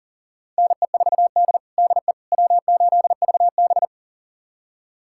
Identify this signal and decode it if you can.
Morse code, heard as a keyed tone, about 40 wpm, 700 Hz. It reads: NE4D DE W8VB
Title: USA Callsign DE Callsign - 4 Characters - 40wpm